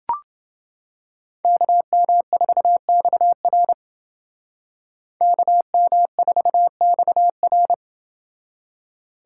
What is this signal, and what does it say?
Morse code, heard as a keyed tone, about 29 wpm, 700 Hz. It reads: KM4XR KM4XR